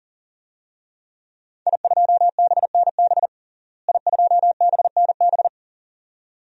Morse code, heard as a keyed tone, about 40 wpm, 700 Hz. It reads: I2BNB I2BNB
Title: DXCC-100 Call Signs - 5 Characters - Repeated 2x - 40wpm